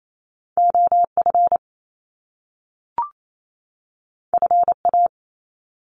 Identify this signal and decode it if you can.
Morse code, heard as a keyed tone, about 28 wpm, 700 Hz. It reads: OF FA
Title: Top Two Letter Combinations in Words 28wpm